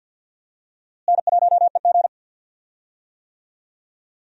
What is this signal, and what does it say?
Morse code, heard as a keyed tone, about 52 wpm, 700 Hz. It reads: N1EG
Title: USA Call Signs - 4 Characters - Repeated 3x - 50wpm